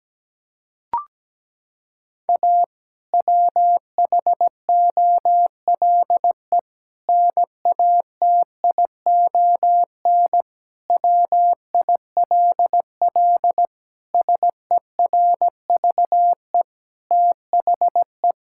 Morse code, heard as a keyed tone, about 17 wpm, 700 Hz. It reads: A WHOLE NATION WILL SERVE THE
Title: Sentences from Top 1000 Words - Review Part 6 - 17wpm